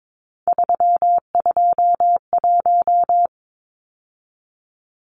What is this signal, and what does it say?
Morse code, heard as a keyed tone, about 22 wpm, 700 Hz. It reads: 321
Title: Numbers - 3-Digit 22wpm